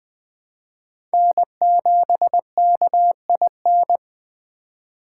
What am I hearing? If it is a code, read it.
Morse code, heard as a keyed tone, about 21 wpm, 700 Hz. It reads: N7KIN
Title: Speed Racing - USA 1x3 Call Signs 17wpm